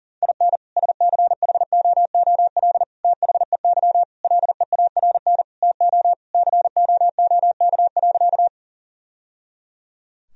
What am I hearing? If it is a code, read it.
Morse code, heard as a keyed tone, about 40 wpm, 700 Hz. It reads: IN SCHOOL THEY LEARN TO COOK.
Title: Sentences from Top 1000 Words - Review Part 4 - 40wpm